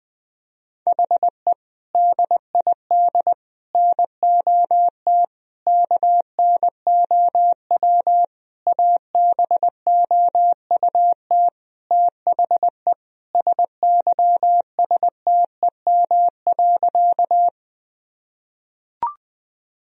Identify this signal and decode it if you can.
Morse code, heard as a keyed tone, about 20 wpm, 700 Hz. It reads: HE DID NOT KNOW ABOUT THE SYSTEM.
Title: Sentences from Top 1000 Words - Review Part 3 - 20wpm